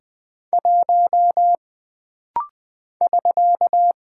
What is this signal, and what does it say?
Morse code, heard as a keyed tone, about 20 wpm, 700 Hz. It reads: 1 <SK>
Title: Single Character (Letter:Number:Punctuation:Prosign) - Rapid-Fire - 20wpm